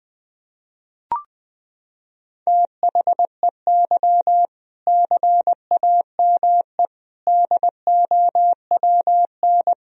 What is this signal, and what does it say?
Morse code, heard as a keyed tone, about 20 wpm, 700 Hz. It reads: THEY CAME DOWN
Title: Sentences from Top 1000 Words - Review Part 1 - 20wpm